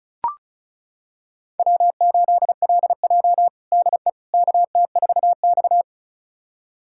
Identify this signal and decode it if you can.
Morse code, heard as a keyed tone, about 34 wpm, 700 Hz. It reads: W8LJ DE KT4X
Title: USA Callsign DE Callsign - 4 Characters - 35wpm